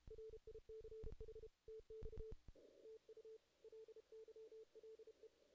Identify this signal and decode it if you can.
Morse code, heard as a keyed tone, about 33 wpm, 440 Hz. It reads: RICH TX TU LYLE
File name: 2016 JAN CW NAQP / 2016 Jan naqp k5ot - 0-6